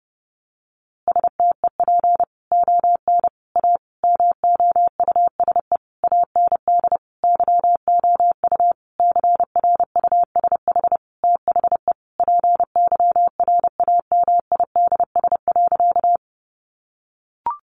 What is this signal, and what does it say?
Morse code, heard as a keyed tone, about 30 wpm, 700 Hz. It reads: STEP ON A MOUSE AND YOU CRUSH THE PYRAMIDS.
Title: A Sound Of Thunder 30wpm